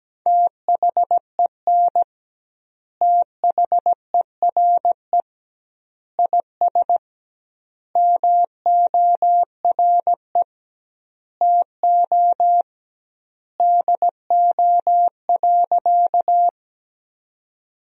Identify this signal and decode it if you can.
Morse code, heard as a keyed tone, about 17 wpm, 700 Hz. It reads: THEN THERE IS MORE TO DO.
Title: 2x Word-Spacing - Sentences from Top 100 Words - 17wpm